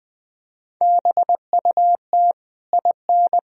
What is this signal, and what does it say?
Morse code, heard as a keyed tone, about 20 wpm, 700 Hz. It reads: BUT IN
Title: Sentences from Top 1000 Words - Review Part 4 - 20wpm